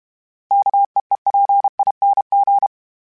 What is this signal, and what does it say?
Morse code, heard as a keyed tone, about 32 wpm, 800 Hz. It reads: KEEPING
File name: Special INGs-mp3s / Special INGs-mp3s / Sngl Ing @30